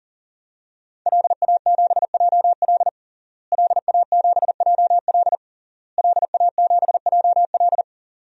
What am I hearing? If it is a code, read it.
Morse code, heard as a keyed tone, about 40 wpm, 700 Hz. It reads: LA7JL LA7JL LA7JL
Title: DXCC-100 Call Signs - 5 Characters - Repeated 3x - 40wpm